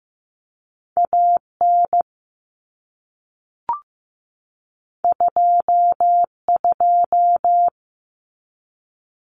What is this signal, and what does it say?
Morse code, heard as a keyed tone, about 15 wpm, 700 Hz. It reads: AN 22
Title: Common Cut Numbers - 2-Digit 15wpm